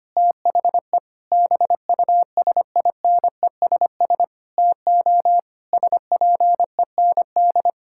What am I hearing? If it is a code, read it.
Morse code, heard as a keyed tone, about 25 wpm, 700 Hz. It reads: THE BUSINESS TO SPEND
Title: Sentences from Top 1000 Words - Review Part 6 - 25wpm